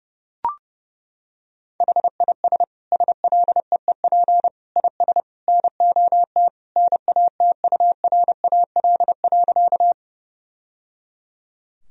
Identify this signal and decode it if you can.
Morse code, heard as a keyed tone, about 30 wpm, 700 Hz. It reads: HIS SLEEP IS NOT NATURAL.
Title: Sentences from Top 1000 Words - Review Part 4 - 30wpm